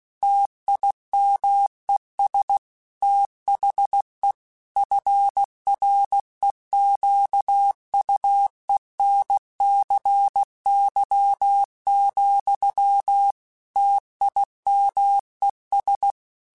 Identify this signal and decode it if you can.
Morse code, heard as a keyed tone, about 16 wpm, 800 Hz. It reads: TIMES THE FREQUENCY, TIMES
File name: Technical Subjects / Technical Subjects-mp3s / XL @15